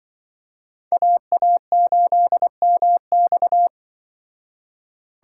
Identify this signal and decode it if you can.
Morse code, heard as a keyed tone, about 25 wpm, 700 Hz. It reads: AA8MX
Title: Speed Racing - USA 2x2 Call Signs 20wpm